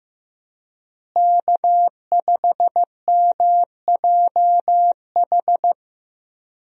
Morse code, heard as a keyed tone, about 15 wpm, 700 Hz. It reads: K5MJH
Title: USA Call Signs - 5 Characters - Repeated 2x - 15wpm